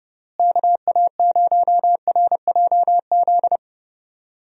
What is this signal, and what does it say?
Morse code, heard as a keyed tone, about 30 wpm, 700 Hz. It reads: KA0RJZ
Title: USA Call Signs - 6 Characters - Repeated 3x - 30wpm